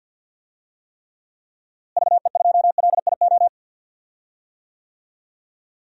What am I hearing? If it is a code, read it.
Morse code, heard as a keyed tone, about 52 wpm, 700 Hz. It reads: VE2LIO
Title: DXCC-100 Call Signs - 6 Characters - Repeated 2x - 50wpm